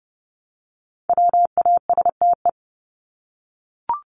Morse code, heard as a keyed tone, about 30 wpm, 700 Hz. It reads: WASTE
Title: Top 1751-2000 Words 30wpm